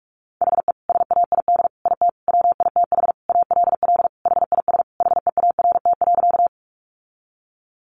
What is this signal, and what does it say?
Morse code, heard as a keyed tone, about 44 wpm, 700 Hz. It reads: HE SAID IT WITH ALL HIS HEART.
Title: Sentences from Top 1200 Words 45wpm